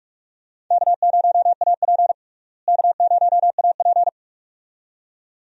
Morse code, heard as a keyed tone, about 43 wpm, 700 Hz. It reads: K0AP K0AP
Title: USA Call Signs - 4 Characters - Repeated 2x - 45wpm